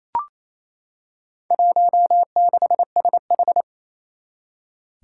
Morse code, heard as a keyed tone, about 28 wpm, 700 Hz. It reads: 16SH
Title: Four Challenging Letter-Number Combinations 28wpm